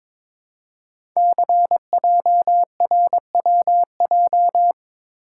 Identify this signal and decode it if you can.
Morse code, heard as a keyed tone, about 22 wpm, 700 Hz. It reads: CJRWJ
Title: Code Groups - 5 Letters - 22wpm